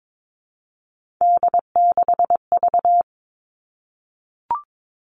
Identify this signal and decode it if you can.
Morse code, heard as a keyed tone, about 22 wpm, 700 Hz. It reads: D6V
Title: Three Challenging Letter-Number Combinations 22wpm